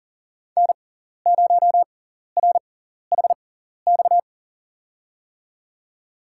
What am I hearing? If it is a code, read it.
Morse code, heard as a keyed tone, about 37 wpm, spaced as for 14 wpm, 700 Hz. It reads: N0RHX
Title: USA Call Signs - 5 Characters - 6x Character Spacing - 40wpm